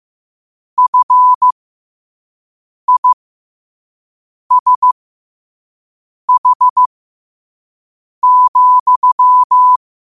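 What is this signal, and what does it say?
Morse code, heard as a keyed tone, about 15 wpm, spaced as for 5 wpm, 1000 Hz. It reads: FISH,
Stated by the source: Morse code quote of the day sent at 5 wpm.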